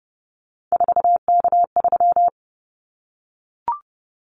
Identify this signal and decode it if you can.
Morse code, heard as a keyed tone, about 30 wpm, 700 Hz. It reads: 4K3
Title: Callsign Prefixes for Top 100 DXCC - 30wpm